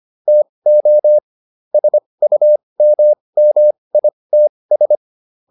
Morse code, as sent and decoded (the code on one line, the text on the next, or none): - --- / ... ..- -- -- .. - ...
TO SUMMITS